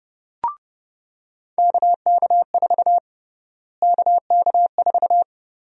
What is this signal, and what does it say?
Morse code, heard as a keyed tone, about 30 wpm, 700 Hz. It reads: KK4 KK4
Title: USA Call Sign Prefixes - Repeated 2x - 30wpm